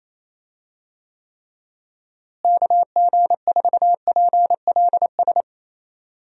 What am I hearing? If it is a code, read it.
Morse code, heard as a keyed tone, about 28 wpm, 700 Hz. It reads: KG4PLS
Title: USA Call Signs - 6 Characters - Repeated 3x - 28wpm